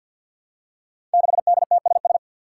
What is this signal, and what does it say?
Morse code, heard as a keyed tone, about 50 wpm, 700 Hz. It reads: 6DTSS
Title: Five Challenging Letter-Number Combinations 50wpm